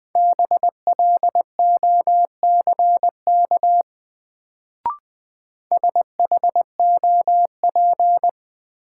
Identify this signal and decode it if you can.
Morse code, heard as a keyed tone, about 20 wpm, 700 Hz. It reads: BLOCK SHOP
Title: Sentences from Top 1600 Words - New Word Drill - 20wpm